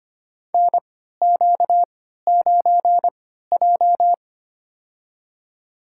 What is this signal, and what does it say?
Morse code, heard as a keyed tone, about 25 wpm, spaced as for 14 wpm, 700 Hz. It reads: NQ9J
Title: USA Call Signs - 4 Characters - 3x Character Spacing - 25wpm